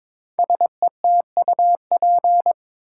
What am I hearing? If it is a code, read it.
Morse code, heard as a keyed tone, about 22 wpm, 700 Hz. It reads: SETUP
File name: 5 Letter Words - First 3 Letters Given - 22wpm